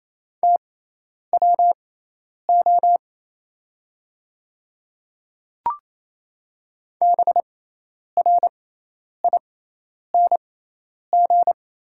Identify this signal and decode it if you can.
Morse code, heard as a keyed tone, about 28 wpm, spaced as for 10 wpm, 690 Hz. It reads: TWO BRING
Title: 6x Character Spacing - Top 500 Words - 28wpm